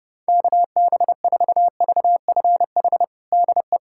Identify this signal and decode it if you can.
Morse code, heard as a keyed tone, about 30 wpm, 700 Hz. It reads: KB4VFH DE
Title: USA Callsign DE Callsign 30wpm